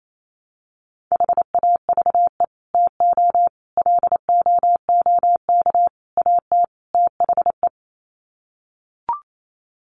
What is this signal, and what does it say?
Morse code, heard as a keyed tone, about 28 wpm, 700 Hz. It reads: HAVE TO LOOK AT THE
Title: Sets of 5 Words Encore 28wpm